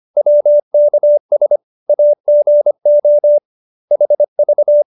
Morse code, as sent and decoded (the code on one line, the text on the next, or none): .-- -.- ... / .- --. --- / .... ...-
WKS AGO HV